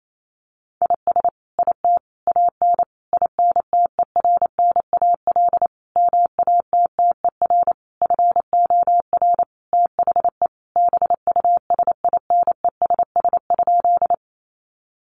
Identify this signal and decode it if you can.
Morse code, heard as a keyed tone, about 28 wpm, 700 Hz. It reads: IS IT AN INTERNAL MATTER FOR THE BUSINESS?
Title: Sentences from Top 1400 Words 28wpm